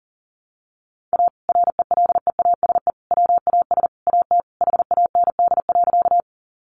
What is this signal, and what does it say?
Morse code, heard as a keyed tone, about 40 wpm, 700 Hz. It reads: A RELEASE WAS AT HAND.
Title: Sentences from Top 1100 Words 40wpm